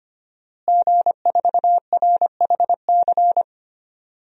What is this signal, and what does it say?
Morse code, heard as a keyed tone, about 25 wpm, 700 Hz. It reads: G4RHC
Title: DXCC-100 Call Signs - 5 Characters - Repeated 3x - 25wpm